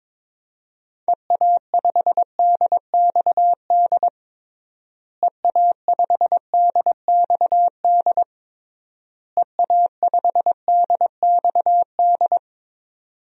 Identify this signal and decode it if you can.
Morse code, heard as a keyed tone, about 21 wpm, 700 Hz. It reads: EA5DXD EA5DXD EA5DXD
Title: DXCC-100 Call Signs - 6 Characters - Repeated 3x - 22wpm